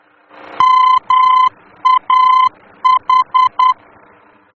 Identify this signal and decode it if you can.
Morse code, heard as a keyed tone, about 10 wpm, 1020 Hz. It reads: MAH
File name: germany / mx / MAH-D-VOR